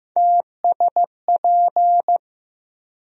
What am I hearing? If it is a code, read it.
Morse code, heard as a keyed tone, about 15 wpm, 700 Hz. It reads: TSP
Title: USA Call Sign Postfixes - 3 Letters 15wpm